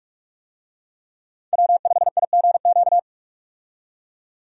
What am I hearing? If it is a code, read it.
Morse code, heard as a keyed tone, about 44 wpm, 700 Hz. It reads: W5IGQ
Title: USA Call Signs - 5 Characters - 45wpm